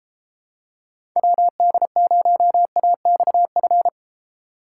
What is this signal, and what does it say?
Morse code, heard as a keyed tone, about 33 wpm, 700 Hz. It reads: WD0AXF